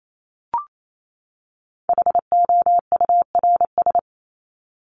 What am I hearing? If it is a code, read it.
Morse code, heard as a keyed tone, about 28 wpm, 700 Hz. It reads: HOURS
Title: ES:S Suffix - Top 100 Words with Base Words 28wpm